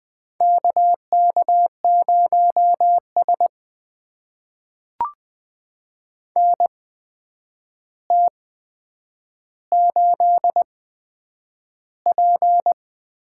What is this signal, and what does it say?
Morse code, heard as a keyed tone, about 20 wpm, spaced as for 9 wpm, 700 Hz. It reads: KK0S NT8P
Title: USA Call Signs - 4 Characters - 8x Character Spacing - 20wpm